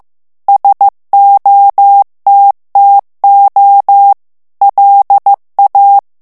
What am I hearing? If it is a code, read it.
Morse code, heard as a keyed tone, about 15 wpm, 800 Hz. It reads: SOTTO LA
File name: ITALIEN texte clair 1 vitesse 14 mots minute